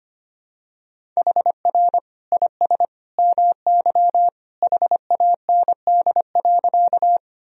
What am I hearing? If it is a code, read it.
Morse code, heard as a keyed tone, about 25 wpm, 700 Hz. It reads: HR IS MY HAND.
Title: Sentences from Top 200 Words and Abbreviations 25wpm